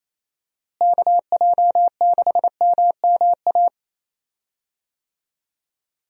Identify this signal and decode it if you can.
Morse code, heard as a keyed tone, about 28 wpm, 700 Hz. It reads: KJ6MMA
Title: USA Call Signs - 6 Characters - Repeated 3x - 28wpm